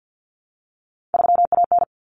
Morse code, heard as a keyed tone, about 50 wpm, 700 Hz. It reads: 3AN
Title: Common Cut Numbers - 3-Digit 50wpm